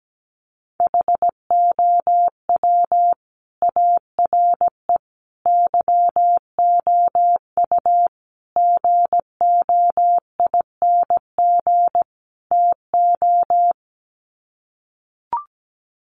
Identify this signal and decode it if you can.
Morse code, heard as a keyed tone, about 17 wpm, 700 Hz. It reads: HOW ARE YOU GOING TO
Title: Sets of 5 Words 17wpm